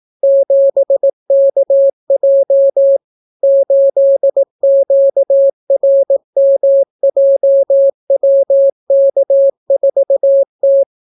Morse code, heard as a keyed tone, about 18 wpm, 550 Hz. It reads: 7KJ 8QRMJWK4T